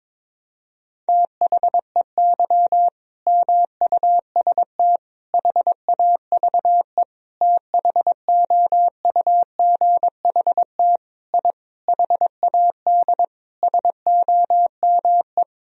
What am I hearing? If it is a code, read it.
Morse code, heard as a keyed tone, about 22 wpm, 700 Hz. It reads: THEY MUST HAVE THOUGHT I HAD SOME
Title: Sentences from Top 1000 Words - Review Part 4 - 22wpm